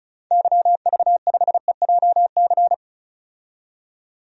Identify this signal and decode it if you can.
Morse code, heard as a keyed tone, about 34 wpm, 700 Hz. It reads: YV5EJC
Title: DXCC-100 Call Signs - 6 Characters - 35wpm